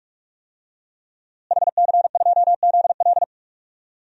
Morse code, heard as a keyed tone, about 46 wpm, 700 Hz. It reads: HC2ZL
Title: DXCC-100 Call Signs - 5 Characters - Repeated 2x - 45wpm